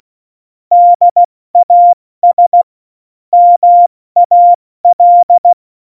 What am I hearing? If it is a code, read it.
Morse code, heard as a keyed tone, about 16 wpm, 700 Hz. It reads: DAS MAL
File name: Morsekurs / Witze / Witz 08. Dezember 2020, 70 F80